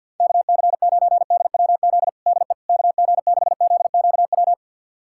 Morse code, heard as a keyed tone, about 50 wpm, 700 Hz. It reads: KC9DWZ DE KG6ZQW
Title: USA Callsign DE Callsign - 6 Characters - 50wpm